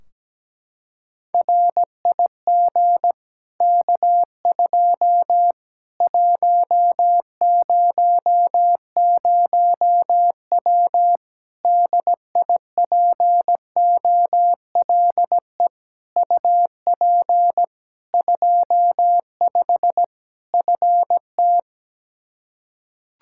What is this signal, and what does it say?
Morse code, heard as a keyed tone, about 17 wpm, 700 Hz. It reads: RIG K2 100W DIPOLE UP 25 FT
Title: Practice QSO - Exchange 2 - Info - 17wpm